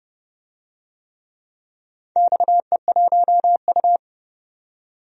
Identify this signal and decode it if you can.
Morse code, heard as a keyed tone, about 30 wpm, 700 Hz. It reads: XE1U
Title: DXCC-100 Call Signs - 4 Characters - 30wpm